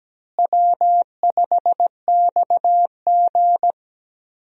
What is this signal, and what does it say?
Morse code, heard as a keyed tone, about 17 wpm, 700 Hz. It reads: W5XG
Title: USA Call Signs - 4 Characters - Repeated 2x - 17wpm